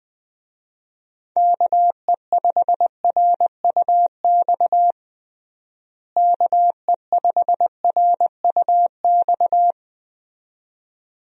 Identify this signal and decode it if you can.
Morse code, heard as a keyed tone, about 21 wpm, 700 Hz. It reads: KE5RUX KE5RUX
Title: USA Call Signs - 6 Characters - Repeated 2x - 20wpm